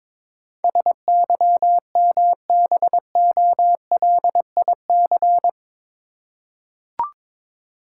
Morse code, heard as a keyed tone, about 21 wpm, 700 Hz. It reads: SYMBOLIC
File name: practice / 22wpm / Eight Letter Words - 22wpm